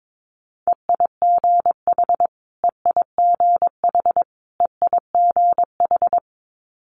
Morse code, heard as a keyed tone, about 22 wpm, 700 Hz. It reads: EIGH EIGH EIGH
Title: Multi-letter Phonemes Repeated 3x 22wpm